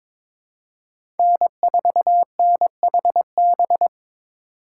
Morse code, heard as a keyed tone, about 22 wpm, 700 Hz. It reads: N4NHB
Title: USA Call Signs - 5 Characters - 2x Character Spacing - 22wpm